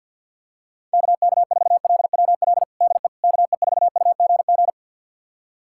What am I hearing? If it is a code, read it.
Morse code, heard as a keyed tone, about 51 wpm, 700 Hz. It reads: KK4LWL DE KE4UGG
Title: USA Callsign DE Callsign - 6 Characters - 50wpm